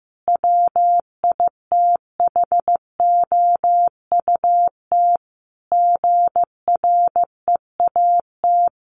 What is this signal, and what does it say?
Morse code, heard as a keyed tone, about 14 wpm, 700 Hz. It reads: WITHOUT GREAT
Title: Sentences from Top 1400 Words 15wpm